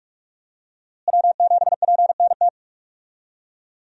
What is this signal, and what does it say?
Morse code, heard as a keyed tone, about 46 wpm, 700 Hz. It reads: W7PNT
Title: USA Call Signs - 5 Characters - Repeated 2x - 45wpm